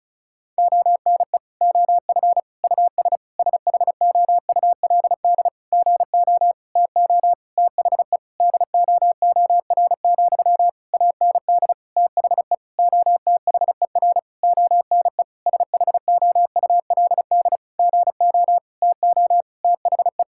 Morse code, heard as a keyed tone, about 36 wpm, 700 Hz. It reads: ONE OF US SHOULD GO TO THE DOOR, AND THE OTHER ONE SHOULD GO TO THE
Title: Sentences from Top 1600 Words 35wpm